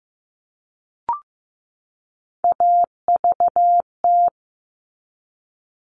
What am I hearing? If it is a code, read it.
Morse code, heard as a keyed tone, about 15 wpm, 700 Hz. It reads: AVT
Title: All Cut Numbers - 3-Digit 15wpm